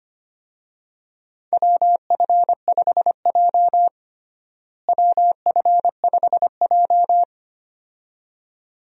Morse code, heard as a keyed tone, about 25 wpm, 700 Hz. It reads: WF5J WF5J
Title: USA Call Signs - 4 Characters - Repeated 2x - 25wpm